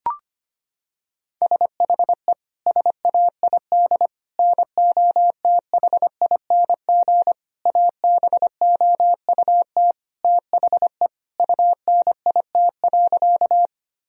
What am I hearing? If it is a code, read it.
Morse code, heard as a keyed tone, about 25 wpm, 700 Hz. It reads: SHE SAID NOTHING ABOUT THE UNIT.
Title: Sentences from Top 1000 Words - Review Part 2 - 25wpm